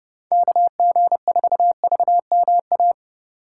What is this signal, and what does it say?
Morse code, heard as a keyed tone, about 30 wpm, 700 Hz. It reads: KG4VMA
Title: USA Call Signs - 6 Characters - 2x Character Spacing - 30wpm